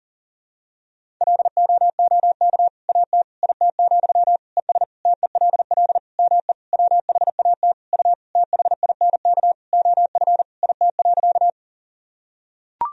Morse code, heard as a keyed tone, about 40 wpm, 700 Hz. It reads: LOOK AT IT, ES TELL ME WHAT U THINK OF IT.
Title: Sentences from Top 200 Words and Abbreviations 40wpm